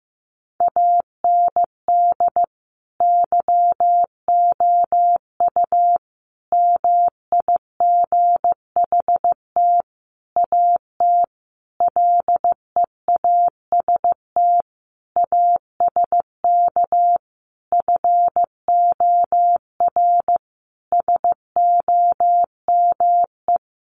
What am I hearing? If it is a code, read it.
Morse code, heard as a keyed tone, about 15 wpm, 700 Hz. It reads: AND YOU MIGHT AT LEAST ASK FOR SOME
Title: Sentences from Top 1500 Words 15wpm